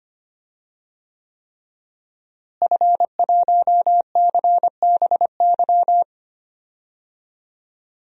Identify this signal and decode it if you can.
Morse code, heard as a keyed tone, about 25 wpm, 700 Hz. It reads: F1CBY
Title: DXCC-100 Call Signs - 5 Characters - Repeated 3x - 25wpm